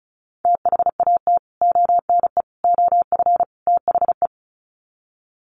Morse code, heard as a keyed tone, about 34 wpm, 700 Hz. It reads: THAT ONE OF THE
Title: Sets of 4 Words 35wpm